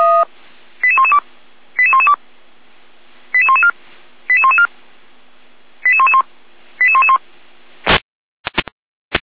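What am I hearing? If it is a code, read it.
Five-tone selective call (ZVEI), heard as quick runs of five tones, tones of about 70 ms. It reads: 88112 88112 88114 88114 88111 88111
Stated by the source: ZVEI five-tone selective call received on SDR.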